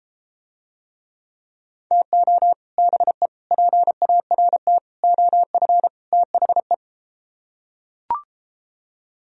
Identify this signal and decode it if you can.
Morse code, heard as a keyed tone, about 33 wpm, 700 Hz. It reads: TO BE PART OF THE